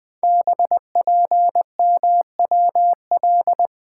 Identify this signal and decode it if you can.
Morse code, heard as a keyed tone, about 20 wpm, 700 Hz. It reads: BPMWL
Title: Code Groups - 5 Letters - 20wpm